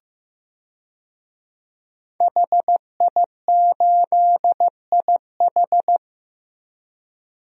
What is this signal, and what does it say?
Morse code, heard as a keyed tone, about 15 wpm, 700 Hz. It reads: HI8IH
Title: DXCC-100 Call Signs - 5 Characters - Repeated 2x - 15wpm